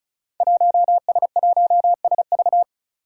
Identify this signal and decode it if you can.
Morse code, heard as a keyed tone, about 36 wpm, 700 Hz. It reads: 1S1SV
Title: Five Challenging Letter-Number Combinations 35wpm